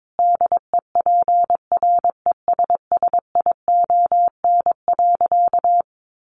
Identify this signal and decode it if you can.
Morse code, heard as a keyed tone, about 22 wpm, 700 Hz. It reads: DEPRESSION.
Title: Sentences from Top 1300 Words 22wpm